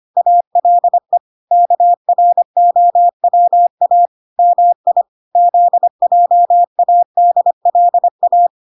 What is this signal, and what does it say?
Morse code, heard as a keyed tone, about 25 wpm, 700 Hz. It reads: ALE KROWA MI ZJADLA